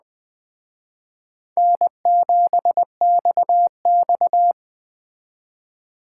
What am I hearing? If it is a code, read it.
Morse code, heard as a keyed tone, about 20 wpm, 700 Hz. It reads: N7XX
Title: Speed Racing - USA 1x2 Call Signs 17wpm